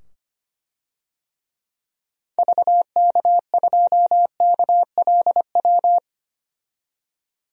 Morse code, heard as a keyed tone, about 25 wpm, 700 Hz. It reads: VK2KLW
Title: DXCC-100 Call Signs - 6 Characters - 25wpm